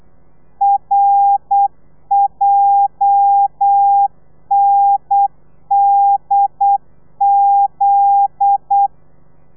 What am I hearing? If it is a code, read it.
Morse code, heard as a keyed tone, about 8 wpm, 800 Hz. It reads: RJNDZ
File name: telegrafia / i2tfj / Lez9 35